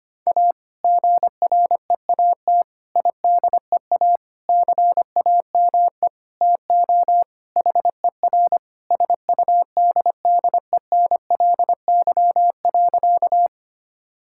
Morse code, heard as a keyed tone, about 25 wpm, 700 Hz. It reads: A GREAT IDEA CAME TO HER SUDDENLY.
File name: practice / 25wpm / Sentences with 2-Word Phrases - 25wpm